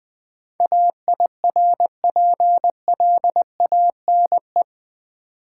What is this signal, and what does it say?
Morse code, heard as a keyed tone, about 20 wpm, 700 Hz. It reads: AIRPLANE
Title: Easy Words with Hints - 20wpm